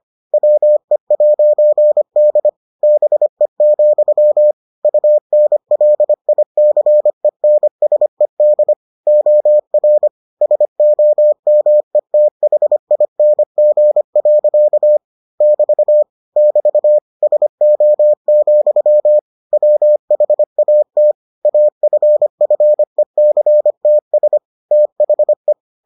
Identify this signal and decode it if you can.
Morse code, heard as a keyed tone, about 25 wpm, 600 Hz. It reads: WE'D BE, UNLICENSED OR SOMETHING. <BT> <BT> SO, WHAT AFFECTS THE